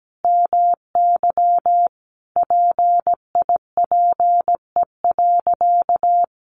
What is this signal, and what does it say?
Morse code, heard as a keyed tone, about 17 wpm, 700 Hz. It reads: MY PIPE.
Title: Sentences from Top 1500 Words 17wpm